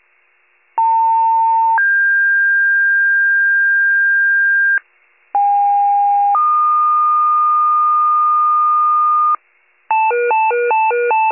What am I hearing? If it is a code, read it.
Two-tone page, heard as a one-second tone, then a three-second tone, about 900 then 1600 Hz, about 810 then 1220 Hz, with noise. It is heard from 0.8 to 9.4 s.
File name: pager / 78 RESCUE DUTY / wcalert